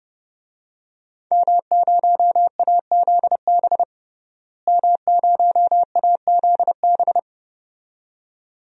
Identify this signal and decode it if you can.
Morse code, heard as a keyed tone, about 30 wpm, 700 Hz. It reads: M0AZB M0AZB
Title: DXCC-100 Call Signs - 5 Characters - Repeated 2x - 30wpm